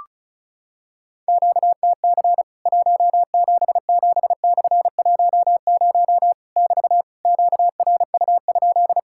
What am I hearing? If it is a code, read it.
Morse code, heard as a keyed tone, about 34 wpm, 700 Hz. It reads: QTC 177/10 <BT> QRU?
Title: Worked All Europe DX Contest - QTC Exchange - 35wpm